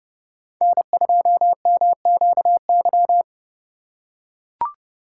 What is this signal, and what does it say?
Morse code, heard as a keyed tone, about 32 wpm, 700 Hz. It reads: N2MQY
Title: USA Call Signs - 5 Characters - 3x Character Spacing - 30wpm